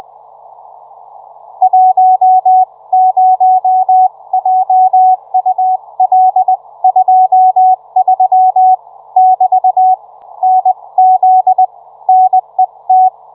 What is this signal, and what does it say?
Morse code, heard as a keyed tone, about 20 wpm, spaced as for 17 wpm, 750 Hz. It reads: 10JUL23<BT> NZ NET